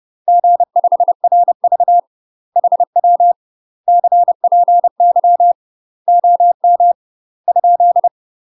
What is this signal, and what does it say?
Morse code, heard as a keyed tone, about 30 wpm, 700 Hz. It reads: G5RV HW CPY OM ?
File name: UK MASTER QSOs TRAINING / 30WPM / UK MASTER QSOs 0036